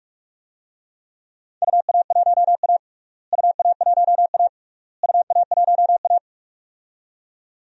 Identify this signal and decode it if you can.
Morse code, heard as a keyed tone, about 46 wpm, 700 Hz. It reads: UA1A UA1A UA1A
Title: DXCC-100 Call Signs - 4 Characters - Repeated 3x - 45wpm